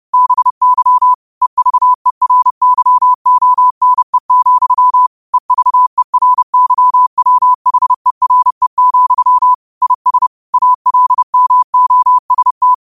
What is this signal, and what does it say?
Morse code, heard as a keyed tone, about 30 wpm, 1000 Hz. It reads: BY EVERYONE, EVERYWHERE, IS ALMOST